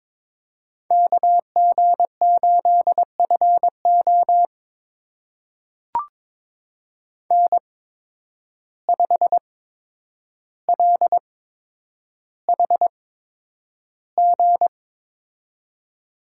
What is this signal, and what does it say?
Morse code, heard as a keyed tone, about 22 wpm, spaced as for 9 wpm, 700 Hz. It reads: KG8FO N5LHG
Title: USA Call Signs - 5 Characters - 8x Character Spacing - 22wpm